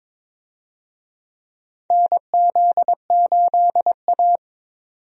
Morse code, heard as a keyed tone, about 22 wpm, 700 Hz. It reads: NZ8A
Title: USA Call Signs - 4 Characters - 22wpm